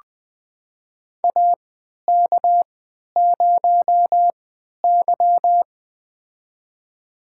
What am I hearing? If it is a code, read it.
Morse code, heard as a keyed tone, about 21 wpm, spaced as for 12 wpm, 700 Hz. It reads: AK0Y
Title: USA Call Signs - 4 Characters - 3x Character Spacing - 20wpm